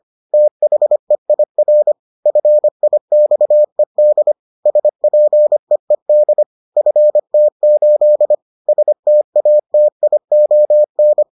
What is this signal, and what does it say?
Morse code, heard as a keyed tone, about 25 wpm, 600 Hz. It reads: THEIR FIXED SPEED FT8 STATION